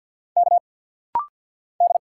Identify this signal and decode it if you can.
Morse code, heard as a keyed tone, about 50 wpm, 700 Hz. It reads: K D
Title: Single Letters - Rapid-Fire 50wpm